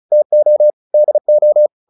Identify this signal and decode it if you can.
Morse code, heard as a keyed tone, about 34 wpm, 600 Hz. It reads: TO DO